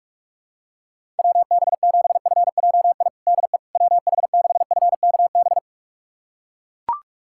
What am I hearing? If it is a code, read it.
Morse code, heard as a keyed tone, about 41 wpm, 700 Hz. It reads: WB7FJI DE WH6FKB
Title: USA Callsign DE Callsign - 6 Characters - 45wpm